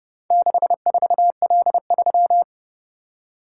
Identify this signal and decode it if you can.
Morse code, heard as a keyed tone, about 30 wpm, 700 Hz. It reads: 64L3